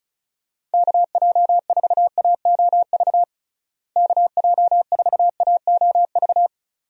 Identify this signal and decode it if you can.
Morse code, heard as a keyed tone, about 34 wpm, 700 Hz. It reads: KJ4AOV KJ4AOV
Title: USA Call Signs - 6 Characters - Repeated 2x - 35wpm